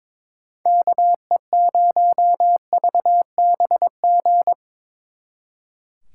Morse code, heard as a keyed tone, about 25 wpm, 700 Hz. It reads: KE0VBG
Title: Speed Racing - USA Call Signs 22wpm